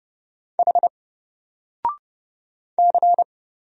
Morse code, heard as a keyed tone, about 31 wpm, 700 Hz. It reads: H C
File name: Single-Letter-Number-30wpm